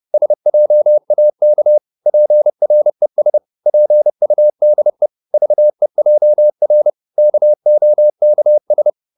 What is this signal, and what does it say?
Morse code, heard as a keyed tone, about 30 wpm, 600 Hz. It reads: SJAK PRES PUDE VEJR KOKS
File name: Korte ord DK 30wpm